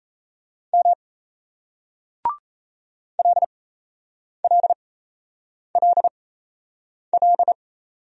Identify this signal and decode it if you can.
Morse code, heard as a keyed tone, about 38 wpm, spaced as for 8 wpm, 700 Hz. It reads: M LLLL